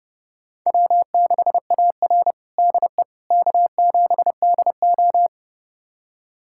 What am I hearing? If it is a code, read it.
Morse code, heard as a keyed tone, about 30 wpm, 700 Hz. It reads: W6AR DE K7DO
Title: USA Callsign DE Callsign - 4 Characters - 30wpm